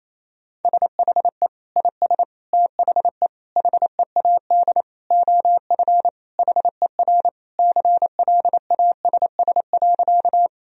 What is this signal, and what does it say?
Morse code, heard as a keyed tone, about 28 wpm, 700 Hz. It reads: SHE IS THE HEAD OF HER CLASS.
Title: Sentences from Top 1000 Words - Review Part 2 - 28wpm